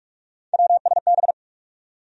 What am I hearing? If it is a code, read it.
Morse code, heard as a keyed tone, about 44 wpm, 700 Hz. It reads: WSB
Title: USA Call Sign Postfixes - 3 Letters 45wpm